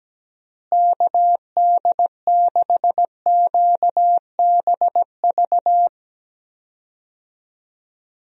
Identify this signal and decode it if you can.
Morse code, heard as a keyed tone, about 17 wpm, 700 Hz. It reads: KD6QBV
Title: USA Call Signs - 6 Characters - Repeated 2x - 17wpm